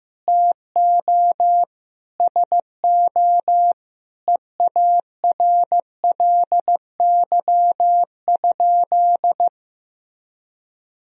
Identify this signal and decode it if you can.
Morse code, heard as a keyed tone, about 15 wpm, 700 Hz. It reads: TO SO EARLY?
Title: Sentences from Top 400 Words and Abbreviations 15wpm